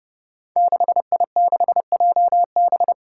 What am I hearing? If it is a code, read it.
Morse code, heard as a keyed tone, about 30 wpm, 700 Hz. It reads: 6I6JB
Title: Five Challenging Letter-Number Combinations 30wpm